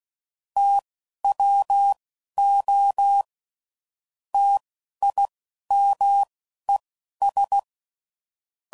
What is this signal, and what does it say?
Morse code, heard as a keyed tone, about 16 wpm, spaced as for 11 wpm, 800 Hz. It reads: TWO TIMES
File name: CW Code Groups @10WPM / Technical Subjects / XL